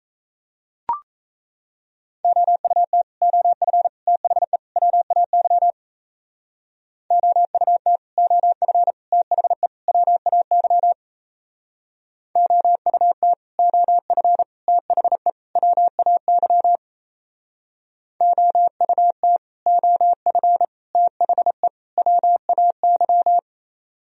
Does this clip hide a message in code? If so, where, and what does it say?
Yes: Morse code from 2.2 to 23.4 s, heard as a keyed tone, about 35 wpm, 700 Hz. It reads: OUT OF THE WAY OUT OF THE WAY OUT OF THE WAY OUT OF THE WAY